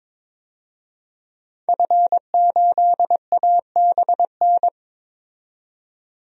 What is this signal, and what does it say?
Morse code, heard as a keyed tone, about 22 wpm, 700 Hz. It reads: F8ABN
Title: DXCC-100 Call Signs - 5 Characters - 22wpm